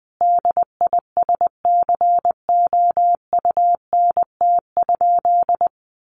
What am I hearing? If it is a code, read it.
Morse code, heard as a keyed tone, about 20 wpm, 700 Hz. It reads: DISCOUNT?
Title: Sentences from Top 1200 Words 20wpm